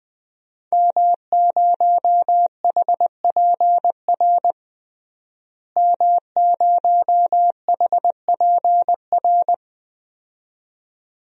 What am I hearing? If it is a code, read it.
Morse code, heard as a keyed tone, about 20 wpm, 700 Hz. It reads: M0HPR M0HPR
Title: DXCC-100 Call Signs - 5 Characters - Repeated 2x - 20wpm